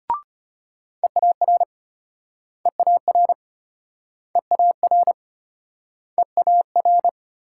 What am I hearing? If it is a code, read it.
Morse code, heard as a keyed tone, about 35 wpm, 700 Hz. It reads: EAR EAR EAR EAR